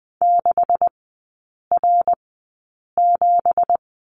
Morse code, heard as a keyed tone, about 20 wpm, spaced as for 8 wpm, 700 Hz. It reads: 6R7
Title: 178 Lesson 29e - Slash - Three Characters 20wpm